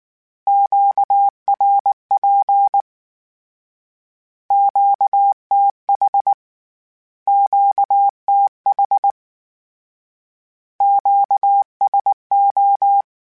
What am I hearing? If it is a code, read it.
Morse code, heard as a keyed tone, about 19 wpm, 800 Hz. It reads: QRP QTH QTH QSO
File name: Q Calls Twice @18